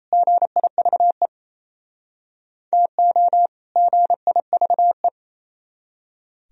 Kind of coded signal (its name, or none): Morse code